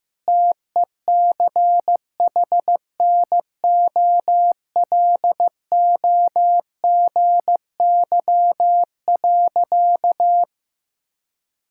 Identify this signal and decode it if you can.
Morse code, heard as a keyed tone, about 15 wpm, 700 Hz. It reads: TECHNOLOGY.
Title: Sentences from Top 1000 Words - Review Part 6 - 15wpm